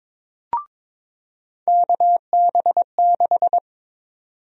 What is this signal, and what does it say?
Morse code, heard as a keyed tone, about 22 wpm, 700 Hz. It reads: KB6
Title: USA Call Sign Prefixes - 22wpm